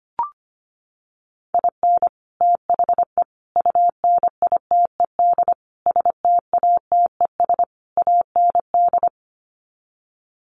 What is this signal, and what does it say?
Morse code, heard as a keyed tone, about 25 wpm, 700 Hz. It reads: IN THE UNITED STATES AND
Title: Sets of 5 Words 25wpm